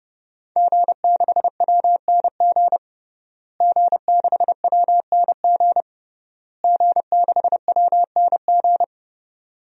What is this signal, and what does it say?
Morse code, heard as a keyed tone, about 32 wpm, 700 Hz. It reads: G6WNG G6WNG G6WNG
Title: DXCC-100 Call Signs - 5 Characters - Repeated 3x - 30wpm